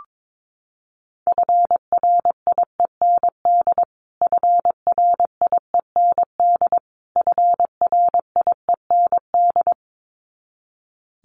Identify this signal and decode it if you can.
Morse code, heard as a keyed tone, about 22 wpm, 700 Hz. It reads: FRIEND FRIEND FRIEND
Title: Top 300 Words Repeated 22wpm